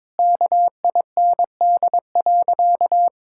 Morse code, heard as a keyed tone, about 22 wpm, 700 Hz. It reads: KIND.
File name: practice / 22wpm / Sentences with 3-Word Phrases - 22wpm